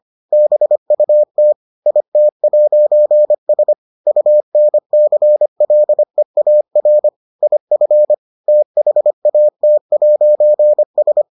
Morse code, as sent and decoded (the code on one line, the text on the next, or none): -... ..- - / .. - .----. ... / ..- -. -.-. .-.. .- .-. / .. ..-. / - .... .- - .----. ...
BUT IT'S UNCLEAR IF THAT'S